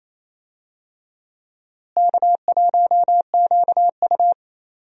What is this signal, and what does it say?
Morse code, heard as a keyed tone, about 28 wpm, 700 Hz. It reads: K1QU
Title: USA Call Signs - 4 Characters - Repeated 3x - 28wpm